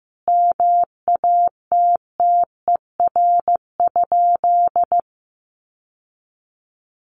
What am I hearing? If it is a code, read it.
Morse code, heard as a keyed tone, about 15 wpm, 700 Hz. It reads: MATTER?
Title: Sentences from Top 700 Words 15wpm